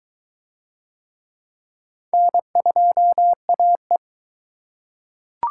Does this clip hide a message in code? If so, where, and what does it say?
Yes: Morse code from 2.1 to 4.0 s, heard as a keyed tone, about 21 wpm, 700 Hz. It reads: N2AE